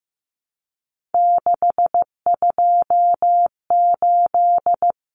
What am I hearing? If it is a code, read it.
Morse code, heard as a keyed tone, about 15 wpm, 700 Hz. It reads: 628
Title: Numbers - 3-Digit 15wpm